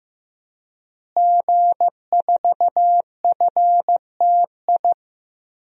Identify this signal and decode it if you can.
Morse code, heard as a keyed tone, about 15 wpm, 700 Hz. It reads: G4FTI
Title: DXCC-100 Call Signs - 5 Characters - Repeated 3x - 15wpm